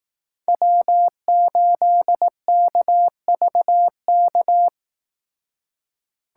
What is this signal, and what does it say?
Morse code, heard as a keyed tone, about 18 wpm, 700 Hz. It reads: W8KVK
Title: Speed Racing - USA 1x3 Call Signs 15wpm